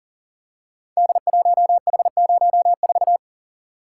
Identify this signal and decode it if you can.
Morse code, heard as a keyed tone, about 38 wpm, 690 Hz. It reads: D1H04
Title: Code Groups - 5 Letter:Numbers - 40wpm